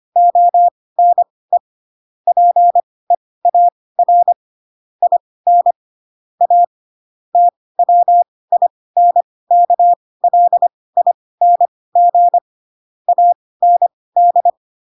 Morse code, heard as a keyed tone, about 25 wpm, spaced as for 18 wpm, 700 Hz. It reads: ONE PEAR IN A TWINKLING AND
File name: Pinocchio / Pinocchio ch7 12 25 ik1hge